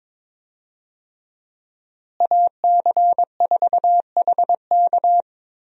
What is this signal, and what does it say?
Morse code, heard as a keyed tone, about 22 wpm, 700 Hz. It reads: AC4HK
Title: USA Call Signs - 5 Characters - Repeated 2x - 22wpm